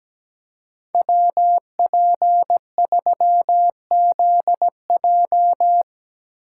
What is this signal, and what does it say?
Morse code, heard as a keyed tone, about 18 wpm, 700 Hz. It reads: WP3ZJ
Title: Speed Racing - USA 2x2 Call Signs 17wpm